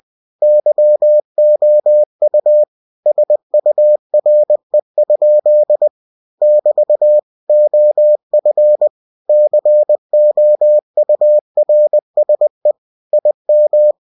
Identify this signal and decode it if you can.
Morse code, heard as a keyed tone, about 20 wpm, 600 Hz. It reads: YOU SURE? <BT> OF COURSE IM
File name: cw-books / winnie-the-pooh / winnie-20wpm-page-0105